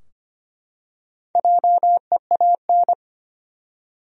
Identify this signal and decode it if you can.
Morse code, heard as a keyed tone, about 25 wpm, 700 Hz. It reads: JEAN
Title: USA Common Names - Max 4 Letters - 25wpm